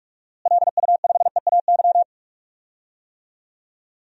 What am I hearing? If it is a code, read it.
Morse code, heard as a keyed tone, about 41 wpm, 700 Hz. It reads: LU5EAY
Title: DXCC-100 Call Signs - 6 Characters - Repeated 3x - 45wpm